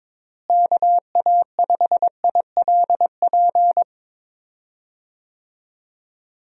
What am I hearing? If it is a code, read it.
Morse code, heard as a keyed tone, about 22 wpm, 700 Hz. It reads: KA5ILP
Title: USA Call Signs - 6 Characters - Repeated 2x - 22wpm